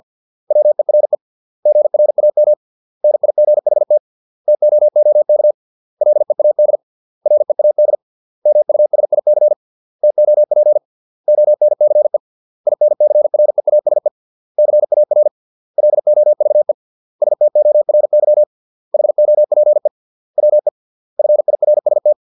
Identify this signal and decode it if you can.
Morse code, heard as a keyed tone, about 50 wpm, 600 Hz. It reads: WERE GRAM NIGHT TOOK LEAD READ MUSIC TOP ONCE INCREASE CAR LOVE STORY HOPE WE FIRST
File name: 632 words / 632 Words 50WPM